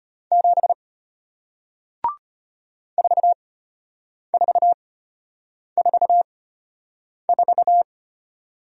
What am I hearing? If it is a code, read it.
Morse code, heard as a keyed tone, about 34 wpm, spaced as for 8 wpm, 700 Hz. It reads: 7 4444